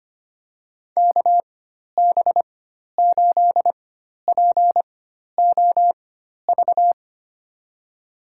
Morse code, heard as a keyed tone, about 25 wpm, spaced as for 12 wpm, 700 Hz. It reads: KB8POV
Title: USA Call Signs - 6 Characters - 4x Character Spacing - 25wpm